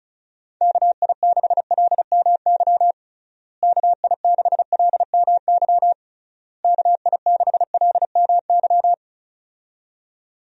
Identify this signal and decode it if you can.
Morse code, heard as a keyed tone, about 34 wpm, 700 Hz. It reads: KI6LMY KI6LMY KI6LMY
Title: USA Call Signs - 6 Characters - Repeated 3x - 35wpm